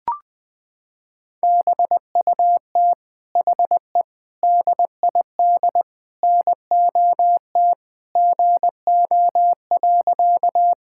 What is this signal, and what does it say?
Morse code, heard as a keyed tone, about 21 wpm, 700 Hz. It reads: BUT HE DID NOT GO.
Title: Sentences with 4-Word Phrases - 20wpm